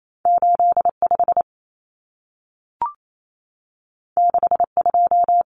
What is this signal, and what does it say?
Morse code, heard as a keyed tone, about 28 wpm, 700 Hz. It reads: 85 62
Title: Numbers - 2-Digit 28wpm